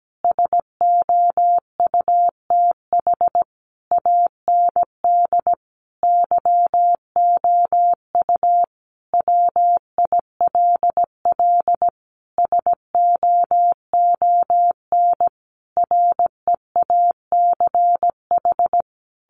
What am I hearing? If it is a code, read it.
Morse code, heard as a keyed tone, about 17 wpm, 700 Hz. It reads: SOUTH AND YOU WILL SOON REACH
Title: Sentences from Top 400 Words 17wpm